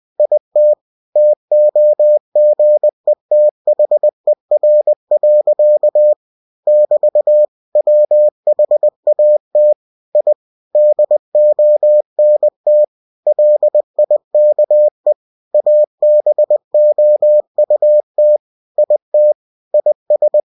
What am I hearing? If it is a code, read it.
Morse code, heard as a keyed tone, about 20 wpm, 600 Hz. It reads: IT TOGETHER. <BT> WHAT I DONT LIKE ABOUT IT IS